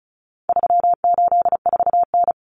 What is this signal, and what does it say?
Morse code, heard as a keyed tone, about 34 wpm, 700 Hz. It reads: 384N
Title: Common Cut Numbers - 4-Digit 35wpm